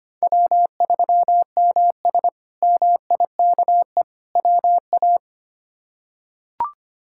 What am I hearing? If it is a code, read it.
Morse code, heard as a keyed tone, about 25 wpm, 700 Hz. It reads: W3MS MIKE WA
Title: K1USN SST QSO Practice - 25wpm